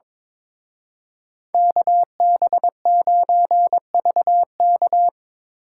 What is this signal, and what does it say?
Morse code, heard as a keyed tone, about 27 wpm, 700 Hz. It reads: KB9VK
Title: Speed Racing - USA 2x2 Call Signs 22wpm